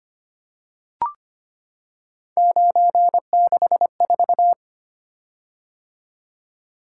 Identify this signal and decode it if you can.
Morse code, heard as a keyed tone, about 25 wpm, 700 Hz. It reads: 964
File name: Three Letter-Number Combinations - 25wpm